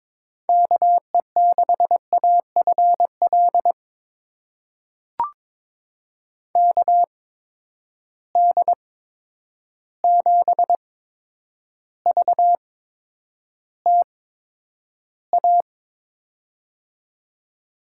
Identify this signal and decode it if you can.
Morse code, heard as a keyed tone, about 22 wpm, spaced as for 9 wpm, 700 Hz. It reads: KE6AFL KD7VTA
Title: USA Call Signs - 6 Characters - 8x Character Spacing - 22wpm